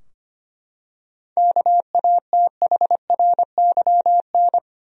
Morse code, heard as a keyed tone, about 25 wpm, 700 Hz. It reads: KATHRYN
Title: USA Common Names - 25wpm